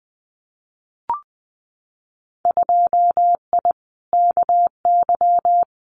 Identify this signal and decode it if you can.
Morse code, heard as a keyed tone, about 20 wpm, 700 Hz. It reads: 2I KY
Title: Winter Field Day Exchanges 20wpm